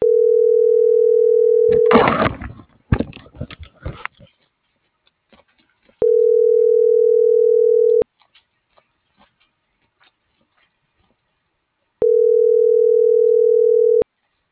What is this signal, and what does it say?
Ringback tone (North America), 440 + 480 Hz.